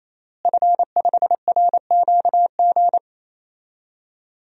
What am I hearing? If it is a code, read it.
Morse code, heard as a keyed tone, about 28 wpm, 700 Hz. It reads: F5RQG
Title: DXCC-100 Call Signs - 5 Characters - Repeated 3x - 28wpm